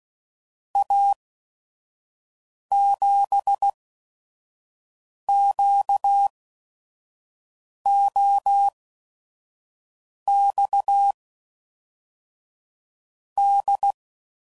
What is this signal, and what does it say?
Morse code, heard as a keyed tone, about 16 wpm, spaced as for 5 wpm, 800 Hz. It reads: A7QOX D
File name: CW Code Groups @5WPM / QSOs / QSO WA7QOX 1